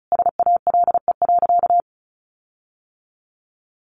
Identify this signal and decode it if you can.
Morse code, heard as a keyed tone, about 34 wpm, 700 Hz. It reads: SALE.
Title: Sentences from Top 1400 Words 35wpm